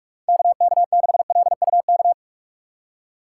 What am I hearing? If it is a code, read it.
Morse code, heard as a keyed tone, about 46 wpm, 700 Hz. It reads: KK6LUK
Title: USA Call Signs - 6 Characters - 6x Character Spacing - 45wpm